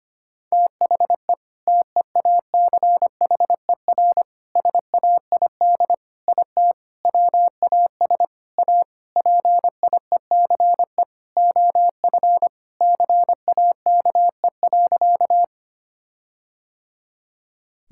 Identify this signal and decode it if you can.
Morse code, heard as a keyed tone, about 25 wpm, 700 Hz. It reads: THE TEACHER SAID IT WAS A PIECE OF CAKE.
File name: practice / 25wpm / All-Age Appropriate Jokes - 25wpm